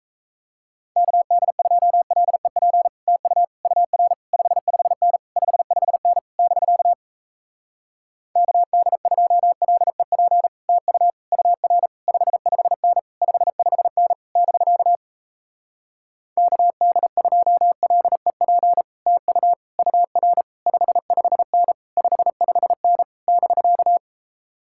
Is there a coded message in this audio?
Yes: Morse code from 1.0 to 24.0 s, heard as a keyed tone, about 38 wpm, 700 Hz. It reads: KD2LEP TU UR 55N 55N <BK> KD2LEP TU UR 55N 55N <BK> KD2LEP TU UR 55N 55N <BK>